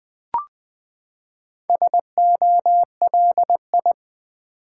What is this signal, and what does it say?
Morse code, heard as a keyed tone, about 20 wpm, 700 Hz. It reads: SOLI
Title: Build A Word - 4 Letters At A Time - 10-Letter Words - 20wpm